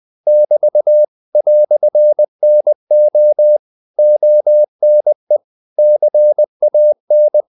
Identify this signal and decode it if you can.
Morse code, heard as a keyed tone, about 20 wpm, 600 Hz. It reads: <BT> "NO ONE CAN
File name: CW / Practice / EnglishQuotations0320